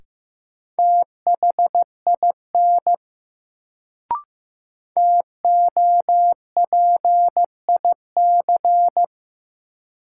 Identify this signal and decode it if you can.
Morse code, heard as a keyed tone, about 15 wpm, 700 Hz. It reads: THIN TOPIC
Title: Sentences from Top 1200 Words - New Word Drill - 15wpm